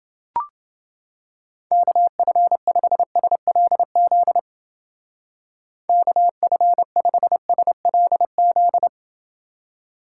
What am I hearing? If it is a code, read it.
Morse code, heard as a keyed tone, about 29 wpm, 700 Hz. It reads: KF5SLZ KF5SLZ